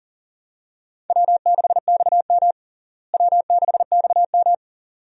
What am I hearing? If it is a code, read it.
Morse code, heard as a keyed tone, about 37 wpm, 700 Hz. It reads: W6XM W6XM
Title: USA Call Signs - 4 Characters - Repeated 2x - 40wpm